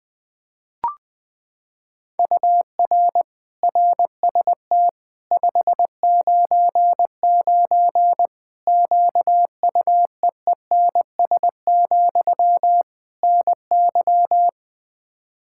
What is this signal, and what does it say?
Morse code, heard as a keyed tone, about 21 wpm, 700 Hz. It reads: UR RST 599 QUEENS, NY
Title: Practice QSO - Exchange 1 - Beginning - 20wpm